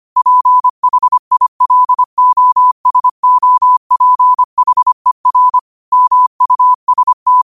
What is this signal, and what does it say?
Morse code, heard as a keyed tone, about 25 wpm, 1000 Hz. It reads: PHILOSOPHER MUST
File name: qotdfiles / 25wpm